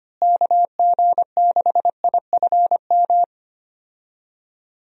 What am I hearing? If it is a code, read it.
Morse code, heard as a keyed tone, about 25 wpm, 700 Hz. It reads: KG6IFM
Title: USA Call Signs - 6 Characters - Repeated 2x - 25wpm